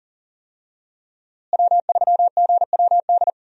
Morse code, heard as a keyed tone, about 40 wpm, 700 Hz. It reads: W3GWD
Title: USA Call Signs - 5 Characters - Repeated 2x - 40wpm